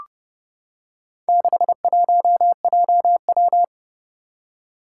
Morse code, heard as a keyed tone, about 30 wpm, 700 Hz. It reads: 61JW
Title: Four Challenging Letter-Number Combinations 30wpm